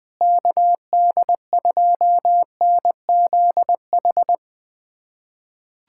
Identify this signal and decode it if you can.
Morse code, heard as a keyed tone, about 20 wpm, 700 Hz. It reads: KD2NZH
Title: USA Callsign DE Callsign 20wpm